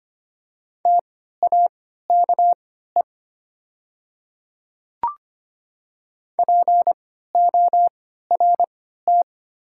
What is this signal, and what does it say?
Morse code, heard as a keyed tone, about 25 wpm, spaced as for 14 wpm, 700 Hz. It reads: TAKE PORT
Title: 3x Character Spacing - Top 200 Words - 25wpm